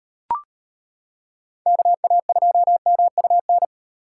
Morse code, heard as a keyed tone, about 35 wpm, 700 Hz. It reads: KA2MUN